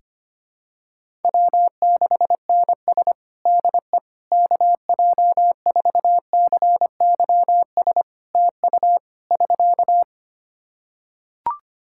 Morse code, heard as a keyed tone, about 25 wpm, 700 Hz. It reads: W6NS DE KJ4CYS TU <SK>
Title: Practice QSO - Exchange 4 - Ending - 25wpm